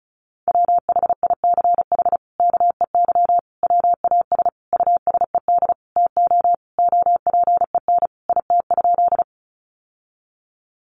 Morse code, heard as a keyed tone, about 34 wpm, 700 Hz. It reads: WHICH KEY WAS USED TO OPEN IT?
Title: Sentences from Top 600 Words 35wpm